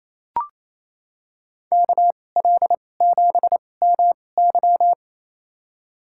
Morse code, heard as a keyed tone, about 28 wpm, spaced as for 20 wpm, 700 Hz. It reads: KL7MY
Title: USA Call Signs - 5 Characters - 2x Character Spacing - 28wpm